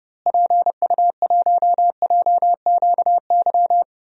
Morse code, heard as a keyed tone, about 32 wpm, 700 Hz. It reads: PU1JQY
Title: DXCC-100 Call Signs - 6 Characters - Repeated 3x - 30wpm